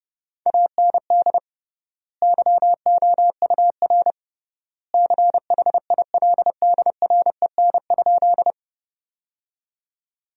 Morse code, heard as a keyed tone, about 30 wpm, 700 Hz. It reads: AND YOUR CHILDREN?
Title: 3x Word-Spacing - Sentences from Top 300 Words - 30wpm